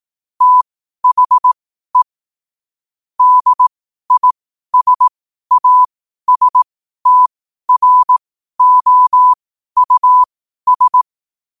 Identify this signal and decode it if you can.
Morse code, heard as a keyed tone, about 18 wpm, spaced as for 12 wpm, 1000 Hz. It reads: THE DISASTROUS
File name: qotdfiles / 12wpm